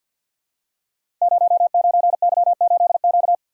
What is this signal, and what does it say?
Morse code, heard as a keyed tone, about 52 wpm, 700 Hz. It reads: 09Y8Q
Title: Code Groups - 5 Letter:Numbers - 50wpm